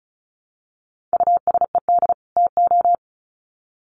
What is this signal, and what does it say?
Morse code, heard as a keyed tone, about 34 wpm, 700 Hz. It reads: USED TO
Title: Sets of 2 Words 35wpm